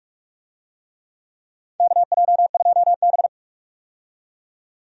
Morse code, heard as a keyed tone, about 47 wpm, 700 Hz. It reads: KJ2B
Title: USA Call Signs - 4 Characters - 45wpm